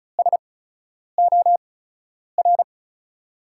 Morse code, heard as a keyed tone, about 36 wpm, spaced as for 10 wpm, 700 Hz. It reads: SOR
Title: USA Call Signs - 5 Characters - 8x Character Spacing - 35wpm